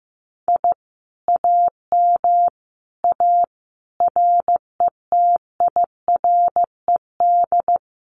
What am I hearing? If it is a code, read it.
Morse code, heard as a keyed tone, about 15 wpm, 700 Hz. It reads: I AM A RETIRED
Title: Common QSO Phrases Encore 15wpm